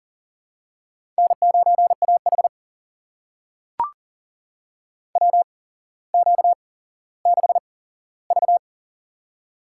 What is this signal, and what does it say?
Morse code, heard as a keyed tone, about 40 wpm, spaced as for 17 wpm, 700 Hz. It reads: N9AH WQ6V
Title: USA Call Signs - 4 Characters - 8x Character Spacing - 40wpm